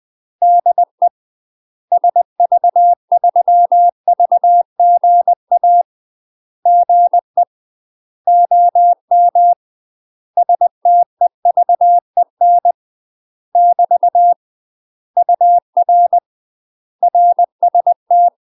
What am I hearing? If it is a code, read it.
Morse code, heard as a keyed tone, about 20 wpm, 700 Hz. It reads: DE SV3VGA GE OM STEVEN <BT> UR RST
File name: UK MASTER QSOs TRAINING / 20WPM / UK MASTER QSOs 0022